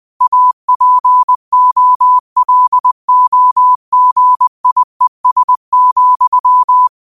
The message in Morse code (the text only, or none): APOLOGIES,